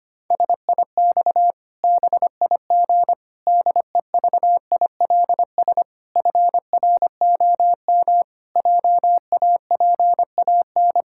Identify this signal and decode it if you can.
Morse code, heard as a keyed tone, about 25 wpm, 700 Hz. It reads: SIX BIG DEVILS FROM JAPAN
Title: Pangram Challenge v2 - 25wpm